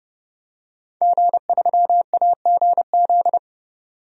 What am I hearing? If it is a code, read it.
Morse code, heard as a keyed tone, about 30 wpm, 700 Hz. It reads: G3AGZ
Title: DXCC-100 Call Signs - 5 Characters - 30wpm